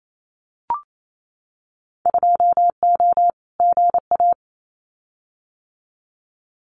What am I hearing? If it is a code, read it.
Morse code, heard as a keyed tone, about 28 wpm, 700 Hz. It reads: 2O GA
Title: Winter Field Day Exchanges Encore 28wpm